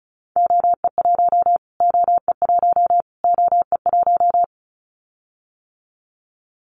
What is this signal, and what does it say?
Morse code, heard as a keyed tone, about 34 wpm, 700 Hz. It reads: OE1 OE1 OE1
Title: Callsign Prefixes for Top 25 DXCC - Repeated 3x - 35wpm